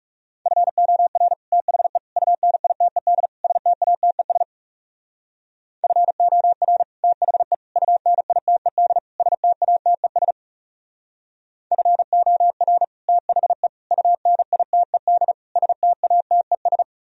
Morse code, heard as a keyed tone, about 37 wpm, 700 Hz. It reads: FOR THE UNITED STATES FOR THE UNITED STATES FOR THE UNITED STATES
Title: Speed-Racing - Sets of 4 Words 30wpm